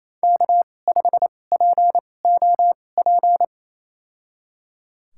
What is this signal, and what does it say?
Morse code, heard as a keyed tone, about 28 wpm, spaced as for 20 wpm, 700 Hz. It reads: K5POP
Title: USA Call Signs - 2x Character Spacing - 28wpm